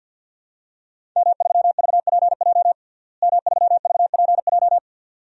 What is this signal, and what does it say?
Morse code, heard as a keyed tone, about 49 wpm, 700 Hz. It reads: M3VPJ M3VPJ
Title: DXCC-100 Call Signs - 5 Characters - Repeated 2x - 50wpm